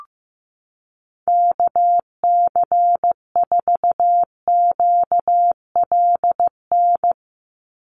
Morse code, heard as a keyed tone, about 15 wpm, 700 Hz. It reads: KC4QLN
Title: USA Call Signs 15wpm